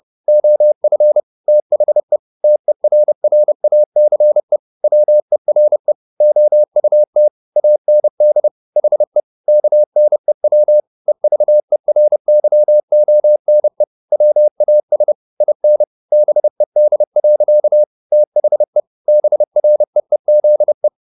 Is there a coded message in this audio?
Yes: Morse code from 0.3 to 20.9 s, heard as a keyed tone, about 30 wpm, 600 Hz. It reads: OF THE TERRACE WERE OUT AND HE KNEW EVERYONE WAS IN BED. THE BREEZE